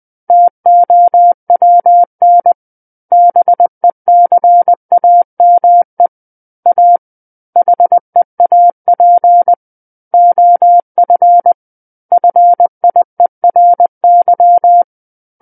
Morse code, heard as a keyed tone, about 20 wpm, 700 Hz. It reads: TOWN BECAME A HEAP OF FIERY
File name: 20 WPM